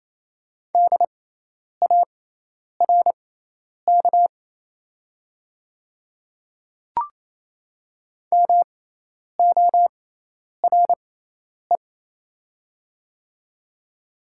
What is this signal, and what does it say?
Morse code, heard as a keyed tone, about 28 wpm, spaced as for 10 wpm, 700 Hz. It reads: DARK MORE
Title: 6x Character Spacing - Top 500 Words - 28wpm